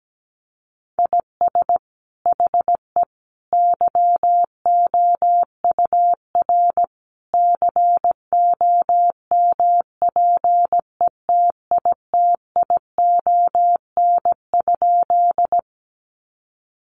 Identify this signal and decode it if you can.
Morse code, heard as a keyed tone, about 17 wpm, 700 Hz. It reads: IS HE YOUR COMPETITION?
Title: Sentences from Top 1000 Words 17wpm